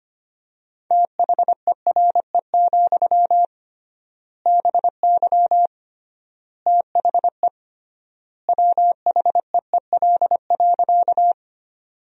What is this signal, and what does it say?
Morse code, heard as a keyed tone, about 25 wpm, 700 Hz. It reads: THERE, BY THE WHEEL.
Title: 3x Word-Spacing - Sentences from Top 500 Words - 25wpm